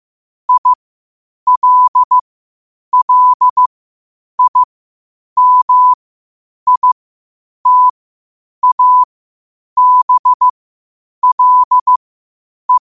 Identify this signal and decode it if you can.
Morse code, heard as a keyed tone, about 15 wpm, spaced as for 8 wpm, 1000 Hz. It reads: ILLIMITABLE
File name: qotdfiles / 8wpm